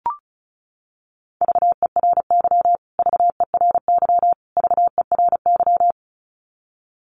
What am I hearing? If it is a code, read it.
Morse code, heard as a keyed tone, about 34 wpm, 700 Hz. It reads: VERY VERY VERY
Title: Top 500 Words Repeated 35wpm